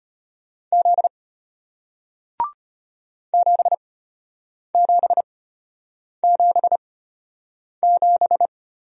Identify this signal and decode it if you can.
Morse code, heard as a keyed tone, about 32 wpm, spaced as for 8 wpm, 700 Hz. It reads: Z 7777